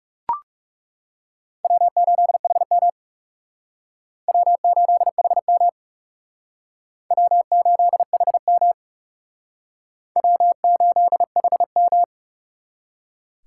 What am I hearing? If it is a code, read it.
Morse code, heard as a keyed tone, about 40 wpm, 700 Hz. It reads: W8HM W8HM W8HM W8HM